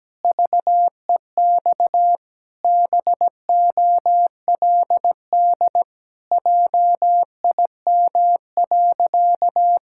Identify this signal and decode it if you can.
Morse code, heard as a keyed tone, about 17 wpm, 700 Hz. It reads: VEX BOLD JIM.
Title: Pangram Challenge v2 - 17wpm